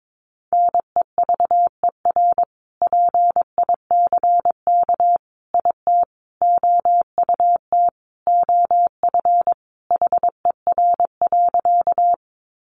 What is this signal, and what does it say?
Morse code, heard as a keyed tone, about 22 wpm, 700 Hz. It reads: NEVER PICK IT OUT OF HER.
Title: Sentences from Top 1000 Words 22wpm